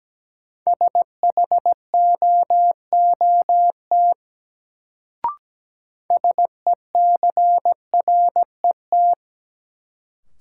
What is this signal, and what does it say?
Morse code, heard as a keyed tone, about 17 wpm, 700 Hz. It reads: SHOOT SECRET
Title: Sentences from Top 1200 Words - New Word Drill - 17wpm